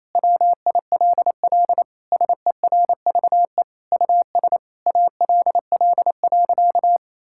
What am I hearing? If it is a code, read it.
Morse code, heard as a keyed tone, about 27 wpm, 700 Hz. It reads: WILL SERVE US ALL.
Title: Sentences from Top 1000 Words - Review Part 5 - 28wpm